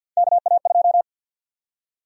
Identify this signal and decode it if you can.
Morse code, heard as a keyed tone, about 52 wpm, 700 Hz. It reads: KA2
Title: USA Call Sign Prefixes - Repeated 2x - 50wpm